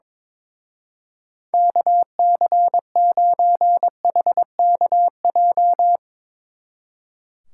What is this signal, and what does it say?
Morse code, heard as a keyed tone, about 22 wpm, 700 Hz. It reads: KC9HKJ
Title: Speed Racing - USA Call Signs 22wpm